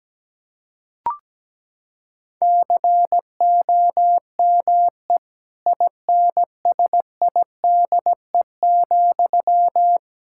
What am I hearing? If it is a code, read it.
Morse code, heard as a keyed tone, about 17 wpm, 700 Hz. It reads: COME INSIDE,
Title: Sentences from Top 1000 Words - Review Part 6 - 17wpm